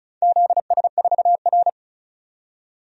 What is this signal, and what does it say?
Morse code, heard as a keyed tone, about 34 wpm, 700 Hz. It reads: ZS4R
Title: DXCC-100 Call Signs - 4 Characters - 35wpm